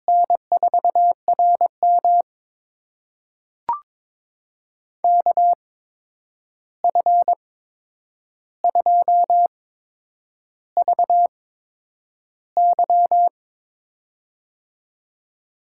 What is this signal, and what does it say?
Morse code, heard as a keyed tone, about 22 wpm, spaced as for 9 wpm, 700 Hz. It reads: N4RM KF2VY
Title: USA Call Signs - 8x Character Spacing - 22wpm